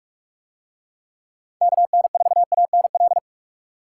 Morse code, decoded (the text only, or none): KN4ANL